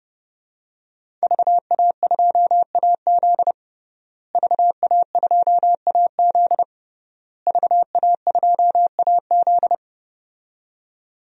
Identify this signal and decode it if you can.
Morse code, heard as a keyed tone, about 30 wpm, 700 Hz. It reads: VA2AZ VA2AZ VA2AZ
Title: DXCC-100 Call Signs - 5 Characters - Repeated 3x - 30wpm